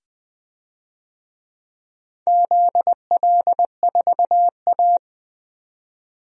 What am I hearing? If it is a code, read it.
Morse code, heard as a keyed tone, about 21 wpm, 700 Hz. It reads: ZL4A
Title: DXCC-100 Call Signs - 4 Characters - 20wpm